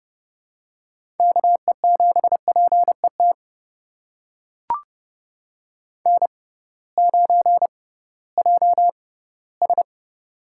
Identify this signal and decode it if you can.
Morse code, heard as a keyed tone, about 29 wpm, spaced as for 17 wpm, 700 Hz. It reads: KE7PET N9JS
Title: USA Call Signs - 6x Character Spacing - 30wpm